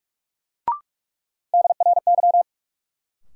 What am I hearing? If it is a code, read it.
Morse code, heard as a keyed tone, about 46 wpm, 700 Hz. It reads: DRY
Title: Sentences from Top 500 Words - New Word Drill - 45wpm